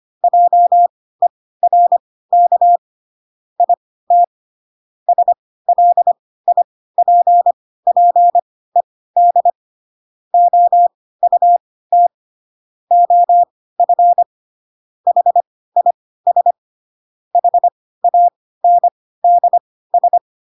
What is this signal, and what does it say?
Morse code, heard as a keyed tone, about 25 wpm, spaced as for 16 wpm, 700 Hz. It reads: JERK IT SLIPPED OUT OF HIS HANDS
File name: Pinocchio / Pinocchio ch2 10 25 ik1hge